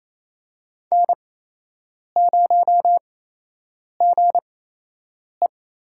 Morse code, heard as a keyed tone, about 29 wpm, spaced as for 8 wpm, 700 Hz. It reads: N0GE
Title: USA Call Signs - 4 Characters - 8x Character Spacing - 28wpm